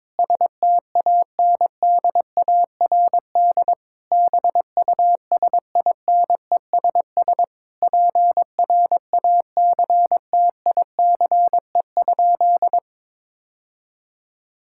Morse code, heard as a keyed tone, about 22 wpm, 700 Hz. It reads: STANDARD BUSINESS PRACTICE?
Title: Sentences from Top 1000 Words - Review Part 6 - 22wpm